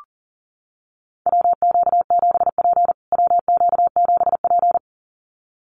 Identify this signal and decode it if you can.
Morse code, heard as a keyed tone, about 40 wpm, 700 Hz. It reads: WQ7P WQ7P
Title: USA Call Signs Repeated 2x 40wpm